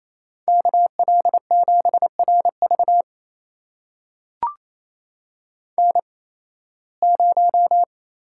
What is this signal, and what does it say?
Morse code, heard as a keyed tone, about 28 wpm, spaced as for 18 wpm, 700 Hz. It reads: KL7RV N0
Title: USA Call Signs - 5 Characters - 8x Character Spacing - 28wpm